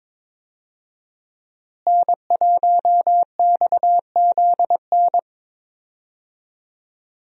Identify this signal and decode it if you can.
Morse code, heard as a keyed tone, about 22 wpm, 700 Hz. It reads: N1XZN
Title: USA Call Signs - 5 Characters - Repeated 2x - 22wpm